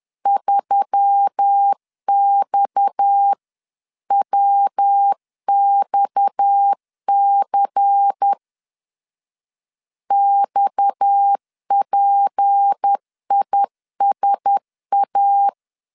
Morse code, heard as a keyed tone, about 10 wpm, 800 Hz. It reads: 3XWXC XPISA
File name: LEZIONI / 6 PKX0 / 6 PKX 10wpm 2426